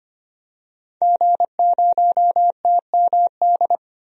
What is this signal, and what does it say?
Morse code, heard as a keyed tone, about 25 wpm, 700 Hz. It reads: G0TMD
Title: DXCC-100 Call Signs - 5 Characters - 25wpm